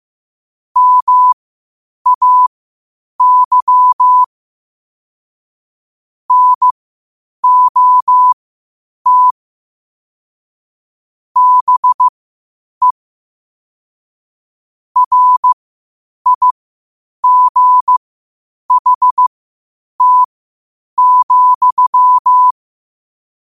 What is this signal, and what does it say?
Morse code, heard as a keyed tone, about 15 wpm, spaced as for 8 wpm, 1000 Hz. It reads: MAY NOT BE RIGHT,
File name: qotdfiles / 8wpm